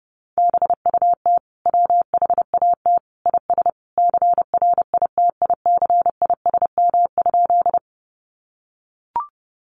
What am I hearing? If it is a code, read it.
Morse code, heard as a keyed tone, about 30 wpm, 700 Hz. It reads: BUT WHAT IS CRITICISM?
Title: Sentences from Top 1300 Words 30wpm